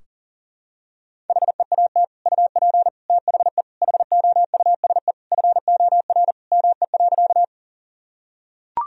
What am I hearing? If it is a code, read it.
Morse code, heard as a keyed tone, about 40 wpm, 700 Hz. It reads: HEAT UP THE HOUSE FOR ME.
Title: Sentences from Top 1000 Words - Review Part 3 - 40wpm